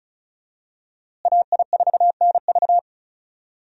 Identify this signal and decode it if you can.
Morse code, heard as a keyed tone, about 36 wpm, 700 Hz. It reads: AI4NV
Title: USA Call Signs - 5 Characters - Repeated 2x - 35wpm